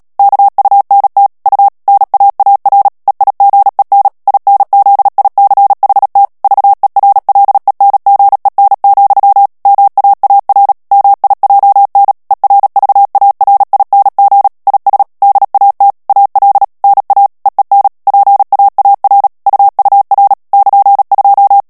Morse code, heard as a keyed tone, about 37 wpm, 800 Hz. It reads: KUNT U NAAR EIGEN INZICHT VERLENGEN, MAAR MIJN ERVARING IS DAT AL NA EEN PAAR UUR <KN>2
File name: morse mp3 / HOLLANDAIS texte clair 2 vitesse 35 mots minute